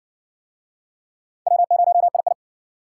Morse code, heard as a keyed tone, about 54 wpm, 700 Hz. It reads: W0II